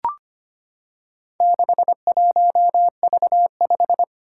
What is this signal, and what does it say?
Morse code, heard as a keyed tone, about 25 wpm, 700 Hz. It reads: 61V5
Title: Four Challenging Letter-Number Combinations 25wpm